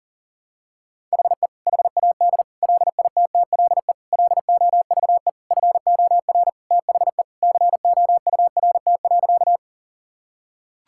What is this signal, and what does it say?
Morse code, heard as a keyed tone, about 40 wpm, 700 Hz. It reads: HE HAD LITTLE LOVE FOR THE COURT.
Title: Sentences from Top 1600 Words 40wpm